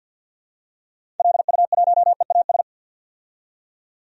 Morse code, heard as a keyed tone, about 51 wpm, 700 Hz. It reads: LU1EAS
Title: DXCC-100 Call Signs - 6 Characters - 50wpm